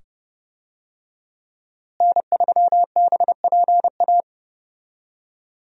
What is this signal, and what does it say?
Morse code, heard as a keyed tone, about 30 wpm, 700 Hz. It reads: N3BPA
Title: USA Call Signs - 5 Characters - Repeated 2x - 30wpm